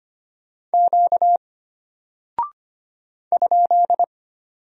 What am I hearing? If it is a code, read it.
Morse code, heard as a keyed tone, about 25 wpm, 700 Hz. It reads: Q ?
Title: Single Character (Letter:Number:Punctuation:Prosign) Alt 25wpm